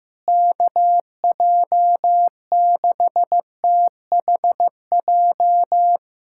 Morse code, heard as a keyed tone, about 15 wpm, 700 Hz. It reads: KJ6THJ
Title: USA Callsign DE Callsign - 6 Characters - 15wpm